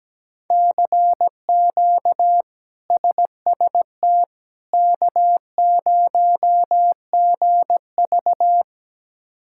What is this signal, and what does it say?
Morse code, heard as a keyed tone, about 17 wpm, 700 Hz. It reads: CQ SST K0GV
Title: K1USN SST QSO Practice - 17wpm